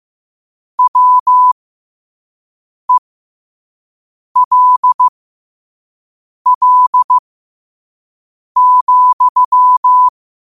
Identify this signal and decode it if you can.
Morse code, heard as a keyed tone, about 15 wpm, spaced as for 5 wpm, 1000 Hz. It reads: WELL,
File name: qotdfiles / 5wpm